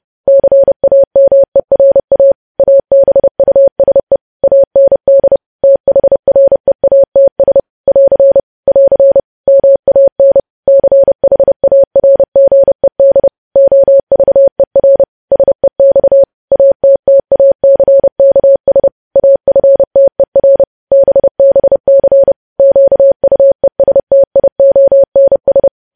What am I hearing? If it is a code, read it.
Morse code, heard as a keyed tone, about 30 wpm, 550 Hz. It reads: CAMERA ABUSE AND THREATS <AR> <AR> MAN CHARGED OVER SEX ATTACKS AFTER BBC QUESTIONS
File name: podcast / News headlines-1x-30x30